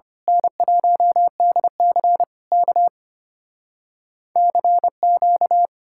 Morse code, heard as a keyed tone, about 24 wpm, 700 Hz. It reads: N1DC K CQ
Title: Speed-Racing Common QSO Phrases - 25wpm